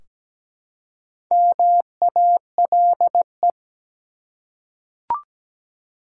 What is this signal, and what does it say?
Morse code, heard as a keyed tone, about 17 wpm, 700 Hz. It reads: MALE
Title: Four Letter Words - 17wpm